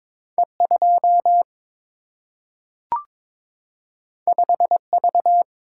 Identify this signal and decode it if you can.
Morse code, heard as a keyed tone, about 22 wpm, 700 Hz. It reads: E 2 5 V
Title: Single-Letter Single-Number 22wpm